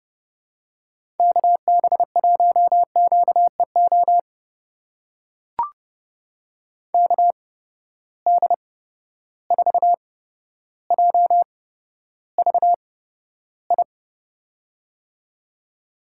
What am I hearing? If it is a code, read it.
Morse code, heard as a keyed tone, about 30 wpm, spaced as for 13 wpm, 700 Hz. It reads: KB1QEO KD4JVI
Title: USA Call Signs - 6 Characters - 8x Character Spacing - 30wpm